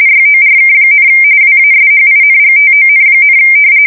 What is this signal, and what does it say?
RTTY, about 45.45 baud. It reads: YORK, WASHINGTON,
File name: fsk170 / battleshipnj